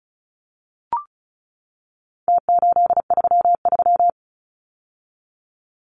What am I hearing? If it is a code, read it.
Morse code, heard as a keyed tone, about 34 wpm, 700 Hz. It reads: T833
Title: Numbers - Contest Style - 4-Digit 35wpm